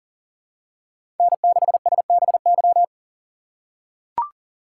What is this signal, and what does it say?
Morse code, heard as a keyed tone, about 40 wpm, 700 Hz. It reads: N6SBY
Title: USA Call Signs - 5 Characters - 2x Character Spacing - 40wpm